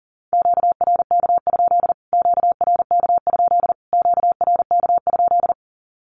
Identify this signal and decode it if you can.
Morse code, heard as a keyed tone, about 40 wpm, 700 Hz. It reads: QRK? QRK? QRK?
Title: Q-Codes Repeated 3x 40wpm